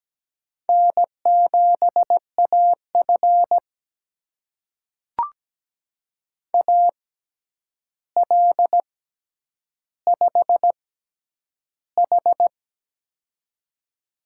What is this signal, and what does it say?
Morse code, heard as a keyed tone, about 17 wpm, spaced as for 9 wpm, 700 Hz. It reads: N7AF AL5H
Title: USA Call Signs - 4 Characters - 6x Character Spacing - 17wpm